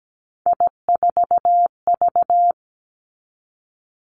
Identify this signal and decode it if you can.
Morse code, heard as a keyed tone, about 17 wpm, 700 Hz. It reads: I4V
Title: Three Challenging Letter-Number Combinations 17wpm